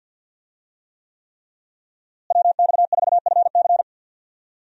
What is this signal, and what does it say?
Morse code, heard as a keyed tone, about 47 wpm, 700 Hz. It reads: WX4FC
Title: USA Call Signs - 5 Characters - Repeated 3x - 50wpm